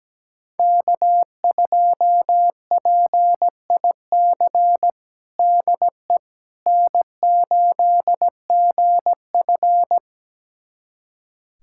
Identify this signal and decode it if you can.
Morse code, heard as a keyed tone, about 17 wpm, 700 Hz. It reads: K2PIC DE N8GF
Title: USA Callsign DE Callsign 17wpm